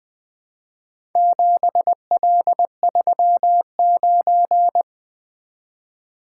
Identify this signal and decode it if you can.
Morse code, heard as a keyed tone, about 20 wpm, 700 Hz. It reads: 7L39
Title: Four Letter-Number Combinations - 20wpm